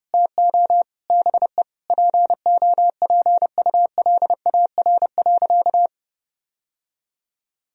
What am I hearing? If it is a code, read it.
Morse code, heard as a keyed tone, about 32 wpm, 700 Hz. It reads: TO BE POPULAR.
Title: Sentences from Top 1000 Words - Review Part 3 - 30wpm